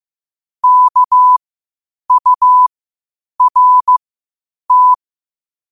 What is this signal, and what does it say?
Morse code, heard as a keyed tone, about 15 wpm, spaced as for 8 wpm, 1000 Hz. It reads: KURT
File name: qotdfiles / 8wpm